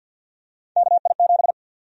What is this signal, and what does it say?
Morse code, heard as a keyed tone, about 53 wpm, 700 Hz. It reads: KI7
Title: USA Call Sign Prefixes - Repeated 2x - 50wpm